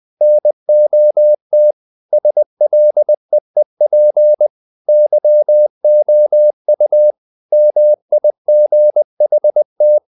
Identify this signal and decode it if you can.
Morse code, heard as a keyed tone, about 20 wpm, 600 Hz. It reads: NOT SLEEP YOU MIGHT
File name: the old man and the sea part 2 20wpm